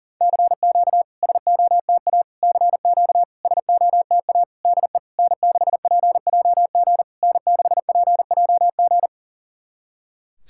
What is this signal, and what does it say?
Morse code, heard as a keyed tone, about 40 wpm, 700 Hz. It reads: CQ SOTA CQ SOTA DE N6PJG N6PJG
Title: SOTA QSO Practice 40wpm